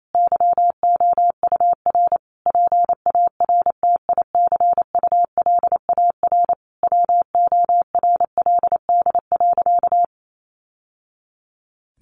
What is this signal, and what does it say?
Morse code, heard as a keyed tone, about 28 wpm, 700 Hz. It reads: YOUR PARTICULAR WORLD.
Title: Sentences from Top 800 Words 28wpm